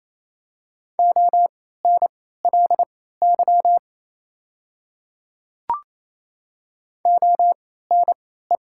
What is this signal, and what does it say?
Morse code, heard as a keyed tone, about 28 wpm, spaced as for 16 wpm, 700 Hz. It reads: ONLY ONE
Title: 3x Character Spacing - Top 200 Words - 28wpm